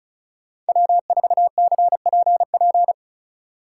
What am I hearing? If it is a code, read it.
Morse code, heard as a keyed tone, about 43 wpm, 700 Hz. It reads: W4CPP
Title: Speed Racing - USA 1x3 Call Signs 35wpm